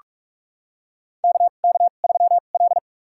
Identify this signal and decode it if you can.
Morse code, heard as a keyed tone, about 44 wpm, spaced as for 32 wpm, 700 Hz. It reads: KK3L
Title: USA Call Signs - 4 Characters - 2x Character Spacing - 45wpm